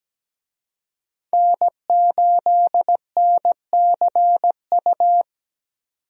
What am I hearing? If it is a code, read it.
Morse code, heard as a keyed tone, about 17 wpm, 700 Hz. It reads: N8NCU
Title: USA Call Signs - 5 Characters - 17wpm